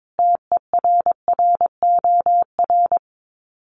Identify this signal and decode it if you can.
Morse code, heard as a keyed tone, about 22 wpm, 700 Hz. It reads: TERROR
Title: Sets of 5 Words Encore2 22wpm